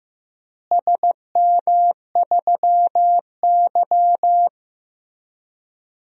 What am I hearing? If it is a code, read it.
Morse code, heard as a keyed tone, about 15 wpm, 700 Hz. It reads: SM3Y
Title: DXCC-100 Call Signs - 4 Characters - Repeated 2x - 15wpm